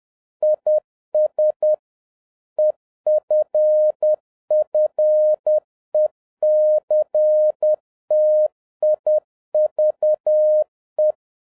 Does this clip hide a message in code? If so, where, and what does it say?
Yes: Morse code from 0.4 to 11.1 s, heard as a keyed tone, about 10 wpm, 610 Hz. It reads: IS EFFECTIVE